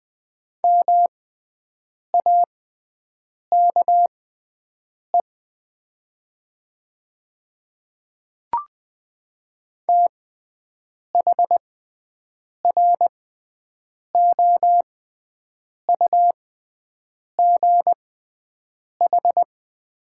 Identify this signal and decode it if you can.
Morse code, heard as a keyed tone, about 20 wpm, spaced as for 7 wpm, 700 Hz. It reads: MAKE THROUGH
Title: 6x Character Spacing - Top 200 Words - 20wpm